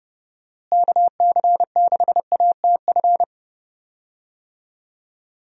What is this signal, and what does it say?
Morse code, heard as a keyed tone, about 30 wpm, 700 Hz. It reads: KC6ATF
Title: USA Call Signs - 6 Characters - Repeated 3x - 30wpm